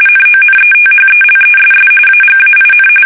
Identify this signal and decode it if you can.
RTTY, about 45.45 baud. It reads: NONETHELESS AGAIN
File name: fsk850 / battleshipnj